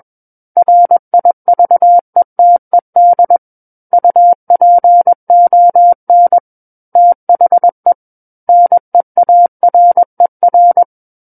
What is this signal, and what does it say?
Morse code, heard as a keyed tone, about 21 wpm, 700 Hz. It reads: RIVETED UPON THE NEARER
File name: Book One / 12-Chapter 12 - 21 WPM